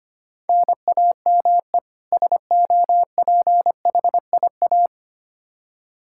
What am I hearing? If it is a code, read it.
Morse code, heard as a keyed tone, about 25 wpm, 700 Hz. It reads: NAME SOPHIA
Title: Practice QSO - Exchange 1 - Beginning - 25wpm